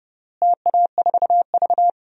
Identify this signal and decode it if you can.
Morse code, heard as a keyed tone, about 29 wpm, 700 Hz. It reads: TA4V
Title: DXCC-100 Call Signs - 4 Characters - Repeated 3x - 30wpm